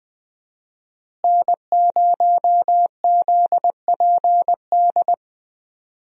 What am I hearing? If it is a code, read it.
Morse code, heard as a keyed tone, about 20 wpm, 700 Hz. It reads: N0ZPD
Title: USA Call Signs - 5 Characters - Repeated 3x - 20wpm